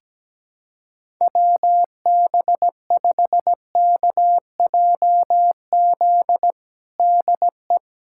Morse code, heard as a keyed tone, about 17 wpm, 700 Hz. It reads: WB5KJZ DE
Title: USA Callsign DE Callsign - 6 Characters - 17wpm